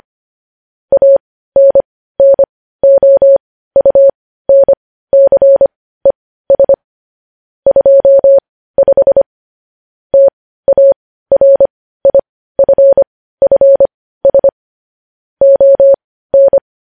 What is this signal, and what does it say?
Morse code, heard as a keyed tone, about 25 wpm, spaced as for 15 wpm, 550 Hz. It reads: ANNOUNCES 25 TARIFFS ON
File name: podcast / News headlines-1x-25x15